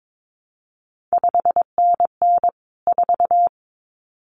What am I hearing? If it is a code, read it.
Morse code, heard as a keyed tone, about 22 wpm, 700 Hz. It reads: 5NN 4
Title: CQ WW DX Contest Exchange Encore - 22wpm